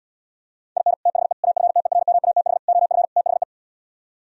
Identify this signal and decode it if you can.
Morse code, heard as a keyed tone, about 78 wpm, 700 Hz. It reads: IN THE AFTERMATH OF THE